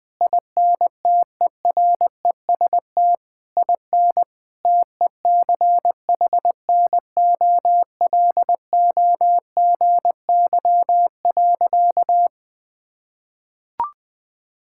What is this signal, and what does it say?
Morse code, heard as a keyed tone, about 21 wpm, 700 Hz. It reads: INTEREST IN TECHNOLOGY.
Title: Sentences from Top 1000 Words - Review Part 5 - 20wpm